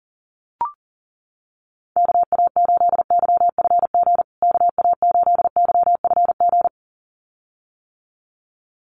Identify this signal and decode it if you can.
Morse code, heard as a keyed tone, about 40 wpm, 700 Hz. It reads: KA8YFG KA8YFG
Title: USA Call Signs Repeated 2x 40wpm